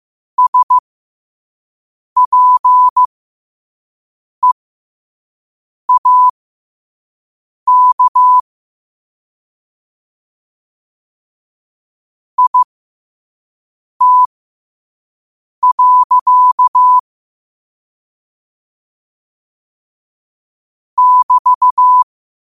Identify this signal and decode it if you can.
Morse code, heard as a keyed tone, about 15 wpm, spaced as for 5 wpm, 1000 Hz. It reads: SPEAK IT. <BT>
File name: qotdfiles / 5wpm